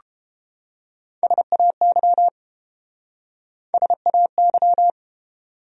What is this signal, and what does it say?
Morse code, heard as a keyed tone, about 31 wpm, 700 Hz. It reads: SAY SAY